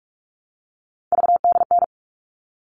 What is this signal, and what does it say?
Morse code, heard as a keyed tone, about 44 wpm, 700 Hz. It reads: VDN
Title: All Cut Numbers - 3-Digit 45wpm